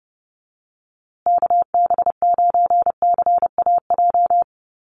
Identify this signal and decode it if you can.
Morse code, heard as a keyed tone, about 30 wpm, 700 Hz. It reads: KB9CAJ
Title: USA Call Signs Repeated 2x 30wpm